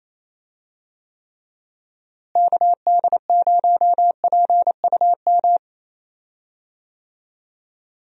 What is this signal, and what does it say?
Morse code, heard as a keyed tone, about 28 wpm, 700 Hz. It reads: KD0PUM
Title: USA Call Signs - 6 Characters - Repeated 3x - 28wpm